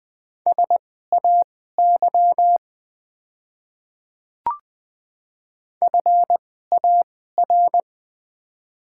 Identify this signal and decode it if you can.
Morse code, heard as a keyed tone, about 20 wpm, spaced as for 15 wpm, 700 Hz. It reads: SAY FAR
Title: 2x Character Spacing - Top 300 Words - 20wpm